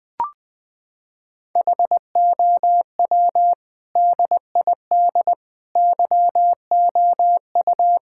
Morse code, heard as a keyed tone, about 20 wpm, 700 Hz. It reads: HOW DID YOU
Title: Sentences with 8-Word Phrases - 20wpm